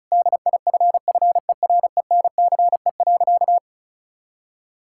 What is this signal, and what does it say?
Morse code, heard as a keyed tone, about 36 wpm, 700 Hz. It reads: DIFFERENCE.
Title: Sentences from Top 1000 Words - Review Part 5 - 35wpm